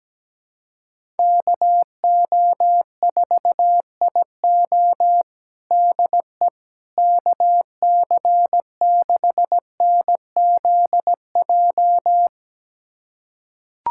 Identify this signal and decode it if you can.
Morse code, heard as a keyed tone, about 17 wpm, 700 Hz. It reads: KO4IO DE KC6NZJ
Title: USA Callsign DE Callsign 17wpm